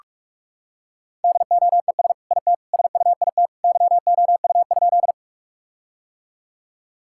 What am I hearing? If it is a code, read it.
Morse code, heard as a keyed tone, about 46 wpm, 700 Hz. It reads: DOES IT SUIT YOU?
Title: Sentences from Top 1600 Words 45wpm